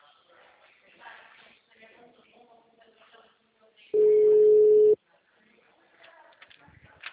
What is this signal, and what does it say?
Ringback tone (Europe, 425 Hz), with speech.
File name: out- -2009- - - .5912